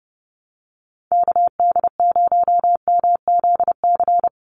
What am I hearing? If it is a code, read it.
Morse code, heard as a keyed tone, about 30 wpm, 700 Hz. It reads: KD0MZC
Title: USA Call Signs 30wpm